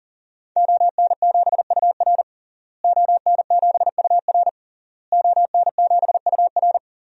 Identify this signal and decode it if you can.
Morse code, heard as a keyed tone, about 40 wpm, 700 Hz. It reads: ON7UR ON7UR ON7UR
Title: DXCC-100 Call Signs - 5 Characters - Repeated 3x - 40wpm